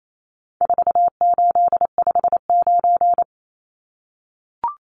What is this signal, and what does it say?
Morse code, heard as a keyed tone, about 28 wpm, 700 Hz. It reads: 4859
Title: Numbers - 4-Digit 28wpm